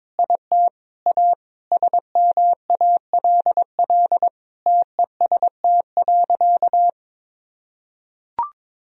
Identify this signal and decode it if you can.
Morse code, heard as a keyed tone, about 22 wpm, 700 Hz. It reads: IT A SMALL TEST.
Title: Sentences from Top 1000 Words - Review Part 3 - 22wpm